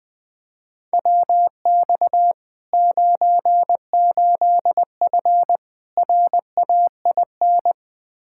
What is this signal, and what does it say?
Morse code, heard as a keyed tone, about 21 wpm, 700 Hz. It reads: WX 98F RAIN
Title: Practice QSO - Exchange 2 - Info - 20wpm